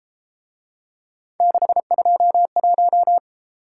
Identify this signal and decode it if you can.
Morse code, heard as a keyed tone, about 30 wpm, 700 Hz. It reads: 621